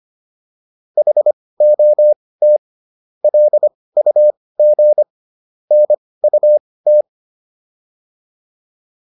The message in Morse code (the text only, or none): HOT LUG NUT